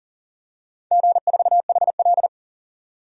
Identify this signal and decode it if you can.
Morse code, heard as a keyed tone, about 37 wpm, 700 Hz. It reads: G4HL
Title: DXCC-100 Call Signs - 4 Characters - 40wpm